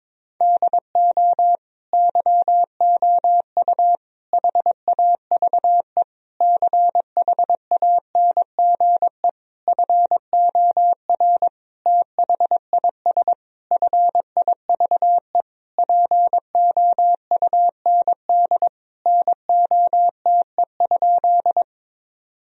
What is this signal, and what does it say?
Morse code, heard as a keyed tone, about 21 wpm, 700 Hz. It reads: DO YOU HAVE CHANGE FOR THIS FIVE POUND NOTE?
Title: Sentences from Top 1000 Words - Review Part 2 - 22wpm